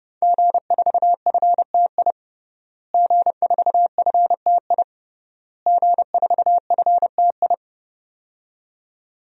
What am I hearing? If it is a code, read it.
Morse code, heard as a keyed tone, about 32 wpm, 700 Hz. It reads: G4FTI G4FTI G4FTI
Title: DXCC-100 Call Signs - 5 Characters - Repeated 3x - 30wpm